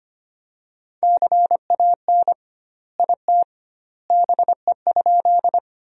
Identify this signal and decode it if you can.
Morse code, heard as a keyed tone, about 25 wpm, 700 Hz. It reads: CAN IT BE?
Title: 2x Word-Spacing - Sentences from Top 100 Words - 25wpm